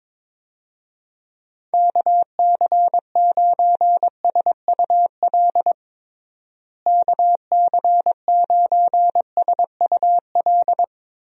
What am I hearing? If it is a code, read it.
Morse code, heard as a keyed tone, about 22 wpm, 700 Hz. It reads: KC9SUL KC9SUL
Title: USA Call Signs - 6 Characters - Repeated 2x - 22wpm